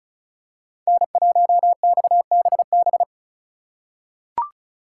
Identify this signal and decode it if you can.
Morse code, heard as a keyed tone, about 36 wpm, 700 Hz. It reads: N1XBB
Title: USA Call Signs - 5 Characters - 3x Character Spacing - 35wpm